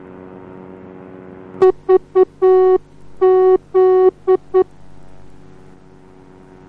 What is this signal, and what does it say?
Morse code, heard as a keyed tone, about 9 wpm, 370 Hz. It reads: VZ